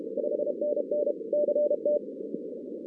Morse code, heard as a keyed tone, about 32 wpm, 590 Hz, with noise. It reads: 5NN CT
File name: 2016 arrl dx cw / 2016-ARRL-DX-CW-K5TR- - 0-46